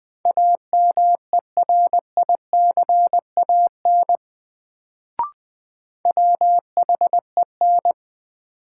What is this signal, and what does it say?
Morse code, heard as a keyed tone, about 20 wpm, 700 Hz. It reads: AMERICAN WHEN
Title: Sets of 2 Words - New Word Drill - 20wpm